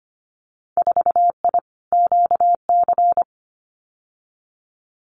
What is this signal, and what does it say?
Morse code, heard as a keyed tone, about 25 wpm, 700 Hz. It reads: 4I QC
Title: Winter Field Day Exchanges Encore 25wpm